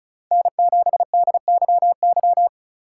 Morse code, heard as a keyed tone, about 36 wpm, 700 Hz. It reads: N7DYY
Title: USA Call Signs - 5 Characters - 3x Character Spacing - 35wpm